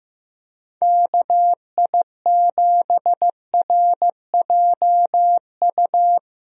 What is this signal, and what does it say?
Morse code, heard as a keyed tone, about 15 wpm, 700 Hz. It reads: KI7RJU
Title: USA Call Signs - 6 Characters - Repeated 3x - 15wpm